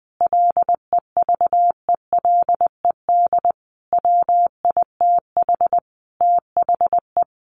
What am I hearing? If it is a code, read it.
Morse code, heard as a keyed tone, about 20 wpm, 700 Hz. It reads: LEVELED WITH THE
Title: A Sound Of Thunder 20wpm